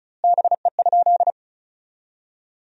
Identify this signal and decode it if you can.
Morse code, heard as a keyed tone, about 34 wpm, 700 Hz. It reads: BE?
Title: 2x Word-Spacing - Sentences from Top 100 Words - 35wpm